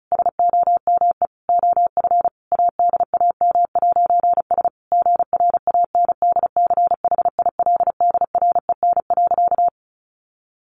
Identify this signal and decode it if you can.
Morse code, heard as a keyed tone, about 34 wpm, 700 Hz. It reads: SOME OF ADAM'S GRANDCHILDREN.
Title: A Sound Of Thunder 35wpm